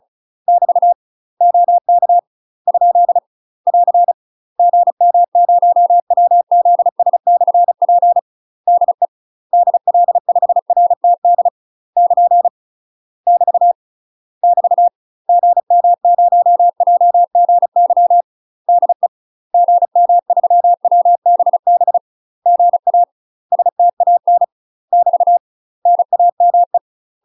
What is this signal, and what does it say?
Morse code, heard as a keyed tone, about 36 wpm, 700 Hz. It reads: <BT> OK ? <AR> GM0WZS/P DE DL5RTD <KN> <BT> <BT> GM0JGY DE GM3WBB GA STAN <BT> NAME
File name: UK MASTER QSOs TRAINING/35WPM